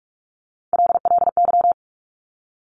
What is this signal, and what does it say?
Morse code, heard as a keyed tone, about 46 wpm, 700 Hz. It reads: LLY
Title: Top Three Letter Combinations in Words 45wpm